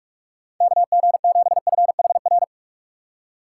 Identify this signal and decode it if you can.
Morse code, heard as a keyed tone, about 46 wpm, 690 Hz. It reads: KG7FHR
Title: USA Call Signs - 6 Characters - 3x Character Spacing - 45wpm